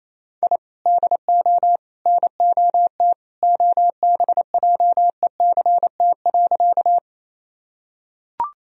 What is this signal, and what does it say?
Morse code, heard as a keyed tone, about 28 wpm, 700 Hz. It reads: I DO NOT OBJECT.
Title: Sentences from Top 1000 Words - Review Part 3 - 28wpm